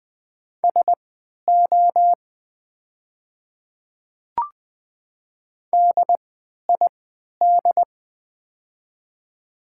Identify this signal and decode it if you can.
Morse code, heard as a keyed tone, about 20 wpm, spaced as for 11 wpm, 690 Hz. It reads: SO DID
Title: 3x Character Spacing - Top 300 Words - 20wpm